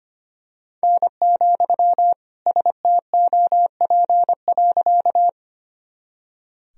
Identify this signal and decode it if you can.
Morse code, heard as a keyed tone, about 25 wpm, 700 Hz. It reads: N, STOP.
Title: Sentences from Top 300 Words and Abbreviations 25wpm